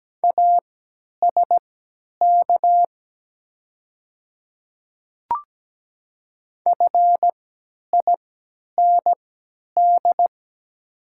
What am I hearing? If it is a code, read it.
Morse code, heard as a keyed tone, about 17 wpm, spaced as for 10 wpm, 700 Hz. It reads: ASK FIND
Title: 3x Character Spacing - Top 200 Words - 17wpm